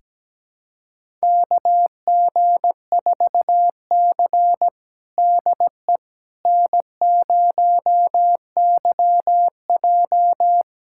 Morse code, heard as a keyed tone, about 17 wpm, 700 Hz. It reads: KG4C DE N0YJ
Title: USA Callsign DE Callsign - 4 Characters - 17wpm